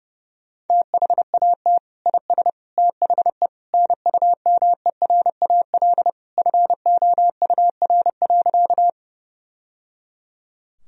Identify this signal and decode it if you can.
Morse code, heard as a keyed tone, about 30 wpm, 700 Hz. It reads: THAT IS THE NUMERAL FOUR.
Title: Sentences from Top 1000 Words - Review Part 2 - 30wpm